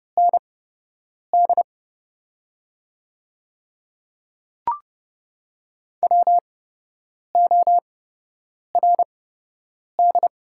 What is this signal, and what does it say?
Morse code, heard as a keyed tone, about 30 wpm, spaced as for 8 wpm, 700 Hz. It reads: ND WORD
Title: 8x Character Spacing - Top 100 Words - 30wpm